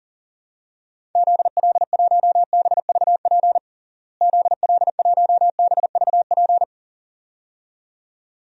Morse code, heard as a keyed tone, about 40 wpm, 700 Hz. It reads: ZL1BVP ZL1BVP
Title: DXCC-100 Call Signs - 6 Characters - Repeated 2x - 40wpm